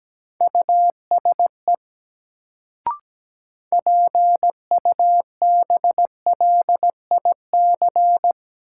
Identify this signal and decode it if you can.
Morse code, heard as a keyed tone, about 17 wpm, 700 Hz. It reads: USE PUBLIC
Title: Sets of 2 Words - New Word Drill - 17wpm